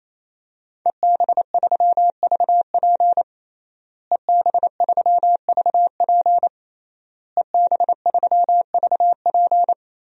Morse code, heard as a keyed tone, about 28 wpm, 700 Hz. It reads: EB3VP EB3VP EB3VP
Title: DXCC-100 Call Signs - 5 Characters - Repeated 3x - 28wpm